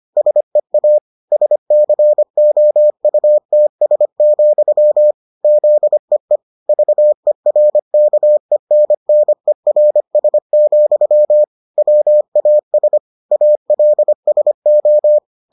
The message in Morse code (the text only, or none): SEA SCOUTS, ZEE VERKENNERS, WAS ALSO